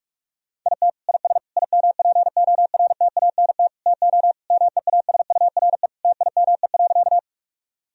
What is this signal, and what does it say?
Morse code, heard as a keyed tone, about 41 wpm, 700 Hz. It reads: IT IS IMPORTANT TO MEASURE TIME.
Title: Sentences from Top 1000 Words - Review Part 2 - 45wpm